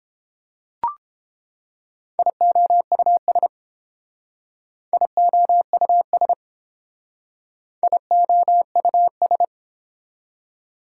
Morse code, heard as a keyed tone, about 29 wpm, 690 Hz. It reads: IOUS IOUS IOUS